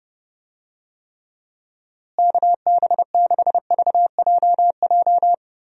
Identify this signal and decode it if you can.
Morse code, heard as a keyed tone, about 30 wpm, 700 Hz. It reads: KB6VJJ
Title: USA Call Signs - 6 Characters - Repeated 3x - 30wpm